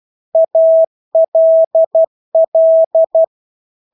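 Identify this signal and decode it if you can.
Morse code, heard as a keyed tone, about 12 wpm, 650 Hz. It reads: ALL
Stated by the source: Morse practice of English words at 12 wpm.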